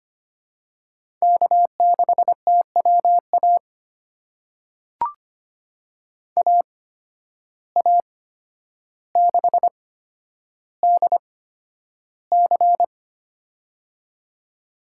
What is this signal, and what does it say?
Morse code, heard as a keyed tone, about 25 wpm, spaced as for 11 wpm, 700 Hz. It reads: K6TWA AA6DC
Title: USA Call Signs - 8x Character Spacing - 25wpm